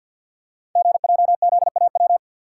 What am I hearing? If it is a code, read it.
Morse code, heard as a keyed tone, about 50 wpm, 690 Hz. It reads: GJZAW
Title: Code Groups - 5 Letters - 50wpm